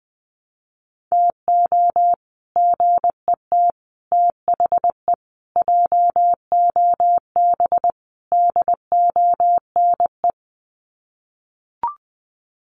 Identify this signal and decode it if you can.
Morse code, heard as a keyed tone, about 20 wpm, 700 Hz. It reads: TO GET THE JOB DONE
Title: Sets of 5 Words Encore2 20wpm